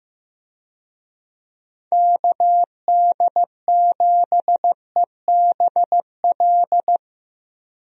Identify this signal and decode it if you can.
Morse code, heard as a keyed tone, about 16 wpm, 700 Hz. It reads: KD7EBL
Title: USA Call Signs - 6 Characters - Repeated 3x - 15wpm